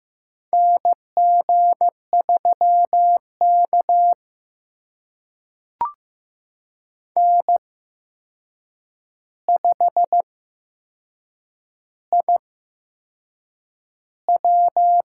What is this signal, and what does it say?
Morse code, heard as a keyed tone, about 15 wpm, spaced as for 6 wpm, 700 Hz. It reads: NG3K N5IW
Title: USA Call Signs - 4 Characters - 8x Character Spacing - 15wpm